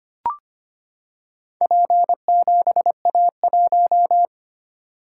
Morse code, heard as a keyed tone, about 25 wpm, 700 Hz. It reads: P7A1
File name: practice / 25wpm / Four Letter-Number Combinations - 25wpm